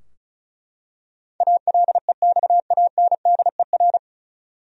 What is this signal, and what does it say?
Morse code, heard as a keyed tone, about 36 wpm, 700 Hz. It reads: ALEXANDER
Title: USA Common Names - 35wpm